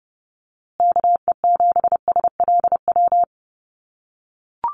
Morse code, heard as a keyed tone, about 30 wpm, 700 Hz. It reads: KE7SLW
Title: USA Call Signs Repeated 2x 30wpm